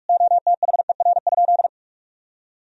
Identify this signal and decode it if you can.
Morse code, heard as a keyed tone, about 46 wpm, 700 Hz. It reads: OTHER?
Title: Sentences from Top 1000 Words - Review Part 1 - 45wpm